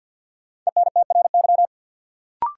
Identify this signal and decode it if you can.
Morse code, heard as a keyed tone, about 53 wpm, 700 Hz. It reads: ENTRY
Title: Sentences from Top 1500 Words - New Word Drill Alt - 50wpm